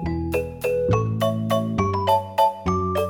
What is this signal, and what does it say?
There is no code.